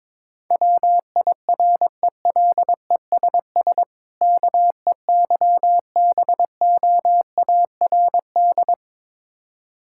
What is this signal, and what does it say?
Morse code, heard as a keyed tone, about 22 wpm, 690 Hz. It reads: WIRELESS KEYBOARD
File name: Difficult Words with Hints - 22wpm